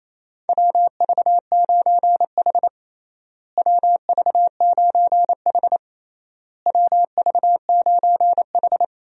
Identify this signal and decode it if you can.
Morse code, heard as a keyed tone, about 28 wpm, 700 Hz. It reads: WV9H WV9H WV9H
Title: USA Call Signs - 4 Characters - Repeated 3x - 28wpm